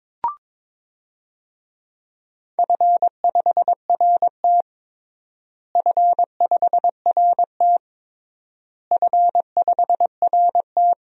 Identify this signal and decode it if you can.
Morse code, heard as a keyed tone, about 22 wpm, 700 Hz. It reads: F5RT F5RT F5RT
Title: DXCC-100 Call Signs - 4 Characters - Repeated 3x - 22wpm